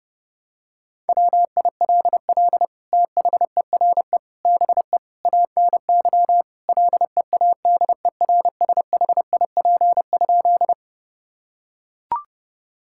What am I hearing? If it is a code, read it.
Morse code, heard as a keyed tone, about 30 wpm, 700 Hz. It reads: WILL THERE BE ANY LEADERSHIP?
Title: Sentences from Top 1600 Words 30wpm